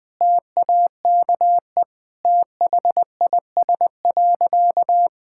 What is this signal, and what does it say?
Morse code, heard as a keyed tone, about 20 wpm, 700 Hz. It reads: TAKE THIS.
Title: Sentences with 2-Word Phrases - Spoken Beginning - 20wpm